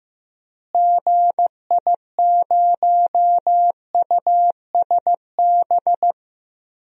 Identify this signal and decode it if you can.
Morse code, heard as a keyed tone, about 16 wpm, 700 Hz. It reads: GI0USB
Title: DXCC-100 Call Signs - 6 Characters - Repeated 3x - 15wpm